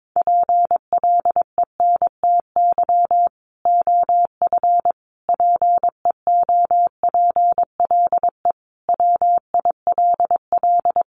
Morse code, heard as a keyed tone, about 22 wpm, 700 Hz. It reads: PLENTY OF PEOPLE WILL
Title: Sentences from Top 1300 Words 22wpm